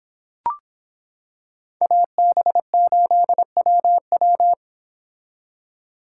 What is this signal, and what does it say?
Morse code, heard as a keyed tone, about 25 wpm, 700 Hz. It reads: AB8WW